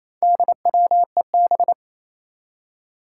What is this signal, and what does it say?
Morse code, heard as a keyed tone, about 27 wpm, 700 Hz. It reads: DWEB
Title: Four Challenging Letter-Number Combinations 28wpm